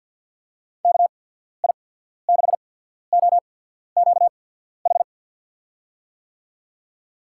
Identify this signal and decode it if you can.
Morse code, heard as a keyed tone, about 53 wpm, spaced as for 14 wpm, 690 Hz. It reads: KI6OQH
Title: USA Call Signs - 6 Characters - 8x Character Spacing - 50wpm